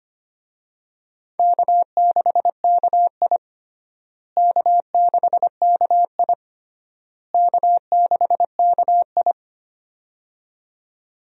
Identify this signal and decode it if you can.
Morse code, heard as a keyed tone, about 25 wpm, 700 Hz. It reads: K6KI K6KI K6KI
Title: USA Call Signs - 4 Characters - Repeated 3x - 25wpm